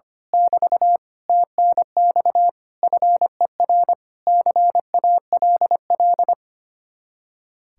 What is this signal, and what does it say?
Morse code, heard as a keyed tone, about 24 wpm, 700 Hz. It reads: <BT> TNX FER CALL
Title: Speed-Racing Common QSO Phrases - 25wpm